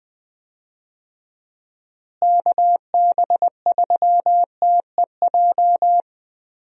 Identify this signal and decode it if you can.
Morse code, heard as a keyed tone, about 21 wpm, 700 Hz. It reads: KB3TEJ
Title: USA Call Signs - 6 Characters - Repeated 2x - 20wpm